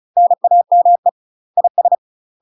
Morse code, heard as a keyed tone, about 36 wpm, 700 Hz. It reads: NAME IS
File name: UK MASTER QSOs TRAINING / 35WPM / UK MASTER QSOs 0023